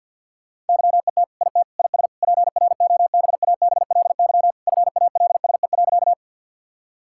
Morse code, heard as a keyed tone, about 50 wpm, 700 Hz. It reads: YET IT IS PROBABLY FALSE.
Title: Sentences from Top 1600 Words 50wpm